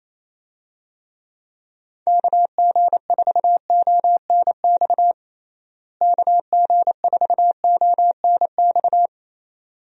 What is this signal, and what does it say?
Morse code, heard as a keyed tone, about 28 wpm, 700 Hz. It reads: KG4ONX KG4ONX
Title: USA Call Signs - 6 Characters - Repeated 2x - 28wpm